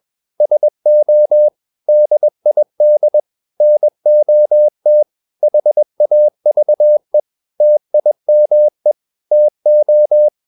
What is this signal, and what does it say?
Morse code, heard as a keyed tone, about 21 wpm, 600 Hz. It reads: SO DID NOT HAVE TIME TO
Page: CW med Gnister